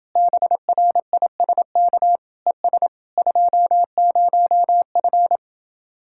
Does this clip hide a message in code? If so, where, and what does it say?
Yes: Morse code from 0.2 to 5.4 s, heard as a keyed tone, about 29 wpm, 700 Hz. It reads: BRISK ES 20F